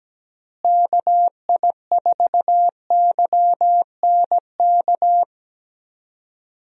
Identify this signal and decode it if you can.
Morse code, heard as a keyed tone, about 17 wpm, 700 Hz. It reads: KI4YNK
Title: USA Call Signs - 6 Characters - Repeated 3x - 17wpm